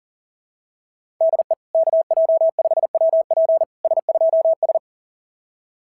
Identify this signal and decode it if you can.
Morse code, heard as a keyed tone, about 38 wpm, 690 Hz. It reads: DE KJ5WP S2S
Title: SOTA QSO Practice 40wpm